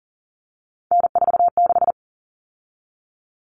Morse code, heard as a keyed tone, about 40 wpm, 700 Hz. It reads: N46
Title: Common Cut Numbers - 3-Digit 40wpm